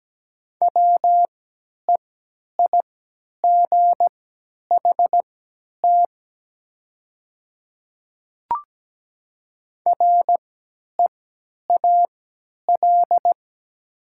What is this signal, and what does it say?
Morse code, heard as a keyed tone, about 17 wpm, spaced as for 10 wpm, 700 Hz. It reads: WEIGHT REAL
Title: 3x Character Spacing - Top 500 Words - 17wpm